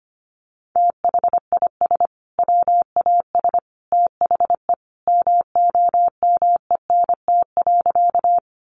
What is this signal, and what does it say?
Morse code, heard as a keyed tone, about 25 wpm, 700 Hz. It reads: THIS WAS THE MOMENT.
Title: Sentences from Top 1000 Words 25wpm